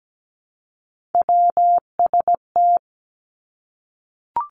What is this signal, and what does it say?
Morse code, heard as a keyed tone, about 17 wpm, 700 Hz. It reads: WST
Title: Three Challenging Letter-Number Combinations 17wpm